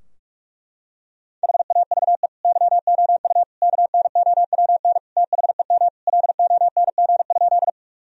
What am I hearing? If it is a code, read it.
Morse code, heard as a keyed tone, about 44 wpm, 700 Hz. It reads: HAVE YOU KNOWN THEM LONG?
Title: Sentences from Top 1000 Words - Review Part 3 - 45wpm